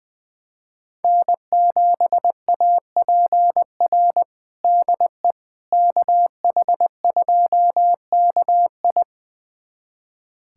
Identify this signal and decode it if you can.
Morse code, heard as a keyed tone, about 20 wpm, 700 Hz. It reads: N7APR DE KH2KI
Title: USA Callsign DE Callsign - 5 Characters - 20wpm